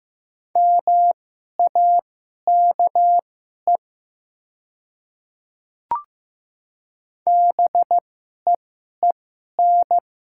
Morse code, heard as a keyed tone, about 15 wpm, spaced as for 11 wpm, 700 Hz. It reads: MAKE BEEN
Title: 2x Character Spacing - Top 200 Words - 15wpm